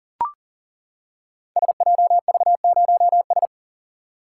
Morse code, heard as a keyed tone, about 37 wpm, 700 Hz. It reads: SJV0S
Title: Code Groups - 5 Letter:Numbers - 40wpm